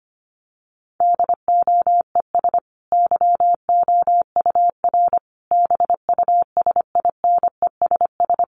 Morse code, heard as a keyed tone, about 25 wpm, 700 Hz. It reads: DOES YOUR BUSINESS
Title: Sentences from Top 1200 Words 25wpm